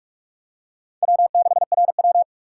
Morse code, heard as a keyed tone, about 45 wpm, 700 Hz. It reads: W6RW
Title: USA Call Signs - 4 Characters - 3x Character Spacing - 45wpm